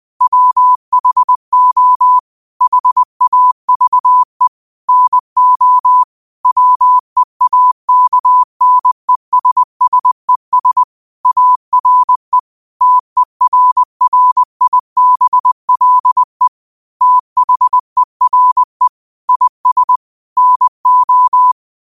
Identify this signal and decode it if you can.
Morse code, heard as a keyed tone, about 20 wpm, 1000 Hz. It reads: WHO HAVE NO WEAKNESSES ARE TERRIBLE THERE IS NO
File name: qotdfiles / 20wpm